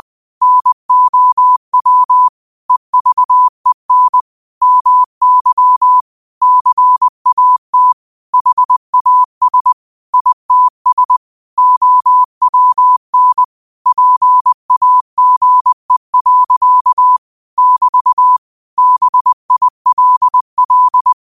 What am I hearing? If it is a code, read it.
Morse code, heard as a keyed tone, about 20 wpm, 1000 Hz. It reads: NOW EVEN MY CAT HAS ITS OWN PAGE. <BT> BILL